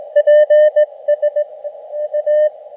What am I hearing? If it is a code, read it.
Morse code, heard as a keyed tone, about 19 wpm, 600 Hz, with radio noise. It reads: PS A
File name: Expedition / SWL UT5NM 20.12.2012 VANGUARD UA1CEG p IT9IFI